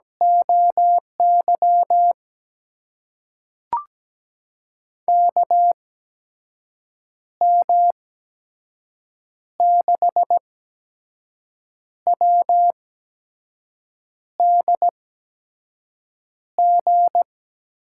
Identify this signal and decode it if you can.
Morse code, heard as a keyed tone, about 17 wpm, spaced as for 5 wpm, 700 Hz. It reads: OY KM6WDG
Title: USA Call Signs - 6 Characters - 8x Character Spacing - 17wpm